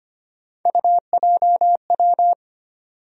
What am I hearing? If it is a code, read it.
Morse code, heard as a keyed tone, about 25 wpm, 700 Hz. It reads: UJW
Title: USA Call Sign Postfixes - 3 Letters 25wpm